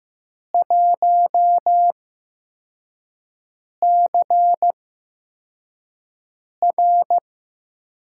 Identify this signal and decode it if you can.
Morse code, heard as a keyed tone, about 15 wpm, spaced as for 4 wpm, 700 Hz. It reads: 1CR
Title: USA Call Signs - 6 Characters - 8x Character Spacing - 15wpm